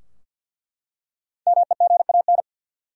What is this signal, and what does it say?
Morse code, heard as a keyed tone, about 47 wpm, 700 Hz. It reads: MEGAN
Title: USA Common Names - 50wpm